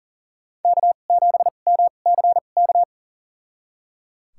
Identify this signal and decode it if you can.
Morse code, heard as a keyed tone, about 39 wpm, spaced as for 28 wpm, 700 Hz. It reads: K7MCK
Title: USA Call Signs - 5 Characters - 2x Character Spacing - 40wpm